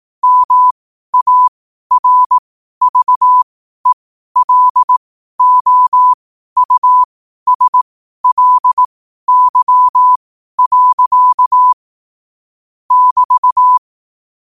Morse code, heard as a keyed tone, about 18 wpm, spaced as for 12 wpm, 1000 Hz. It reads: MARVELOUSLY. <BT>
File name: qotdfiles / 12wpm